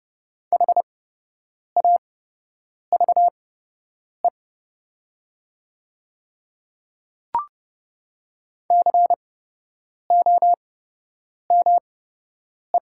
Morse code, heard as a keyed tone, about 30 wpm, spaced as for 8 wpm, 700 Hz. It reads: HAVE COME
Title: 8x Character Spacing - Top 200 Words - 30wpm